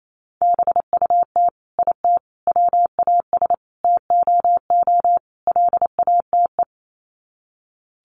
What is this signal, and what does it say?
Morse code, heard as a keyed tone, about 28 wpm, 700 Hz. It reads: BUT IT WAS TOO LATE
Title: Sets of 5 Words Encore2 28wpm